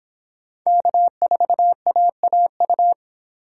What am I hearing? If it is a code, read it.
Morse code, heard as a keyed tone, about 24 wpm, 700 Hz. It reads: K4AAU